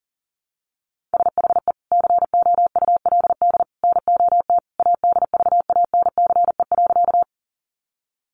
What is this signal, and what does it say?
Morse code, heard as a keyed tone, about 40 wpm, 700 Hz. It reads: SHE COULD NOT ADVANCE.
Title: Sentences from Top 1400 Words 40wpm